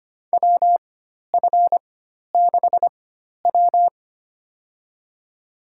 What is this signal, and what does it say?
Morse code, heard as a keyed tone, about 25 wpm, spaced as for 12 wpm, 700 Hz. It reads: WF6W
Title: USA Call Signs - 4x Character Spacing - 25wpm